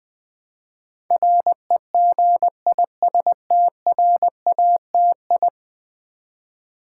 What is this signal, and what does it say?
Morse code, heard as a keyed tone, about 20 wpm, 700 Hz. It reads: REGISTRATI
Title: Build A Word - 5 Letters At A Time - 12-Letter Words - 20wpm